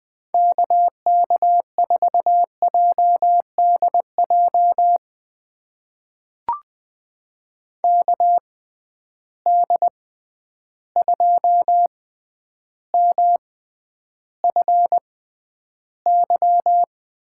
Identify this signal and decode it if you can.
Morse code, heard as a keyed tone, about 20 wpm, spaced as for 10 wpm, 700 Hz. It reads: KK4JDJ KD2MFY
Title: USA Call Signs - 6 Characters - 6x Character Spacing - 20wpm